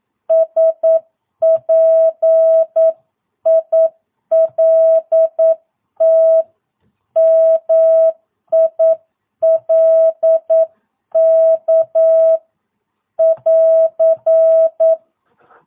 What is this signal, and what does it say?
Morse code, heard as a keyed tone, about 9 wpm, 660 Hz. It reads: SPILT MILK <AR>
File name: jh8cbh / 2022cw